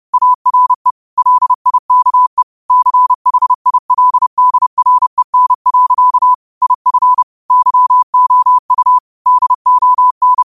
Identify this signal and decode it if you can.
Morse code, heard as a keyed tone, about 30 wpm, 1000 Hz. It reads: ARE LIKE CHILDREN. IF YOU DON
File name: qotdfiles / 30wpm